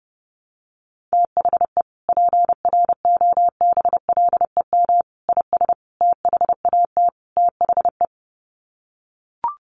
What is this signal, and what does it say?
Morse code, heard as a keyed tone, about 30 wpm, 700 Hz. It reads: THE PROBLEM IS THAT THE
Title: Sets of 5 Words Encore2 30wpm